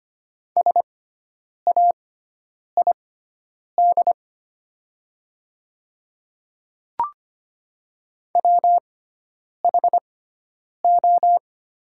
Morse code, heard as a keyed tone, about 25 wpm, spaced as for 9 wpm, 700 Hz. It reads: SAID WHO
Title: 6x Character Spacing - Top 100 Words - 25wpm